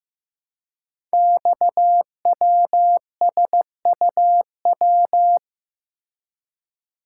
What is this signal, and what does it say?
Morse code, heard as a keyed tone, about 15 wpm, 700 Hz. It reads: XWSUW
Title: Code Groups - 5 Letters - 15wpm